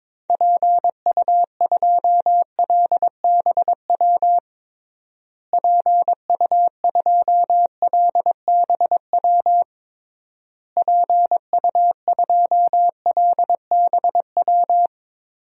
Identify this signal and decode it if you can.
Morse code, heard as a keyed tone, about 22 wpm, 700 Hz. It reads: PU2LBW PU2LBW PU2LBW
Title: DXCC-100 Call Signs - 6 Characters - Repeated 3x - 22wpm